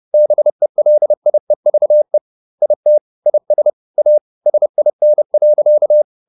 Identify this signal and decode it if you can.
Morse code, heard as a keyed tone, about 30 wpm, 600 Hz. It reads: BELIEVE IT IS A SIN.